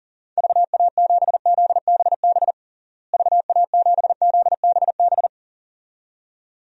Morse code, heard as a keyed tone, about 40 wpm, 700 Hz. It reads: VA7ZBB VA7ZBB
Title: DXCC-100 Call Signs - 6 Characters - Repeated 2x - 40wpm